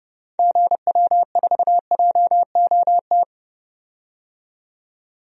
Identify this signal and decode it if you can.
Morse code, heard as a keyed tone, about 30 wpm, 700 Hz. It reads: GW4JOT
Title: DXCC-100 Call Signs - 6 Characters - 30wpm